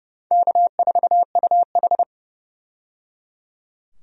Morse code, heard as a keyed tone, about 30 wpm, 700 Hz. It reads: K4UH
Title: USA Call Signs - 4 Characters - 30wpm